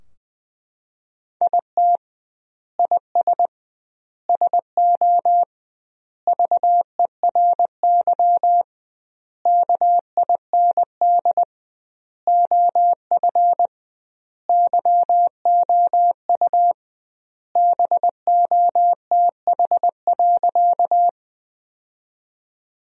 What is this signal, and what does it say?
Morse code, heard as a keyed tone, about 20 wpm, 700 Hz. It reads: IT IS SO VERY KIND OF YOU BOTH.
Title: 2x Word-Spacing - Sentences from Top 300 Words - 20wpm